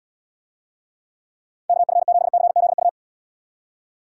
Morse code, heard as a keyed tone, about 79 wpm, 700 Hz. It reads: KF7CZU